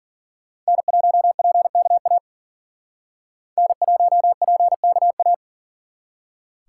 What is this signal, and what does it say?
Morse code, heard as a keyed tone, about 51 wpm, 700 Hz. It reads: N1PKA N1PKA